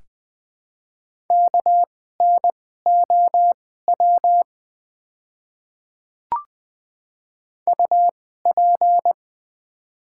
Morse code, heard as a keyed tone, about 20 wpm, spaced as for 14 wpm, 700 Hz. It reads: KNOW UP
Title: 2x Character Spacing - Top 200 Words - 20wpm